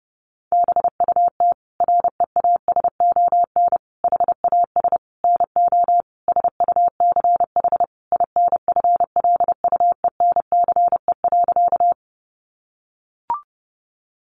Morse code, heard as a keyed tone, about 30 wpm, 700 Hz. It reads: BUT REASON HAS NO SUCH INFLUENCE.
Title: Sentences from Top 1100 Words 30wpm